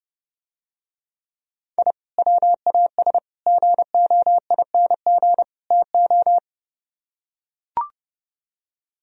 Morse code, heard as a keyed tone, about 31 wpm, 700 Hz. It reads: I WAS GOING TO